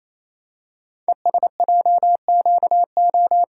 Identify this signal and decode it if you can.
Morse code, heard as a keyed tone, about 29 wpm, 700 Hz. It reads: ESJQO
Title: Code Groups - 5 Letters - 28wpm